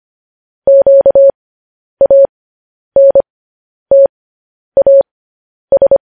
Morse code, heard as a keyed tone, about 25 wpm, spaced as for 10 wpm, 550 Hz. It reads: QANTAS
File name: podcast / News headlines-1x-25x10